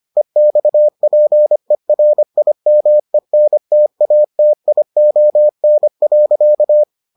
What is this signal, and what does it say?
Morse code, heard as a keyed tone, about 25 wpm, 600 Hz. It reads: EXPERIMENTATION.